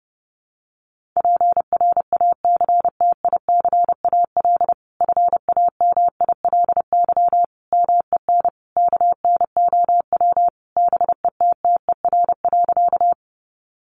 Morse code, heard as a keyed tone, about 30 wpm, 700 Hz. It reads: PRACTICAL FAMILY MEN KNOW BETTER.
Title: Sentences from Top 1400 Words 30wpm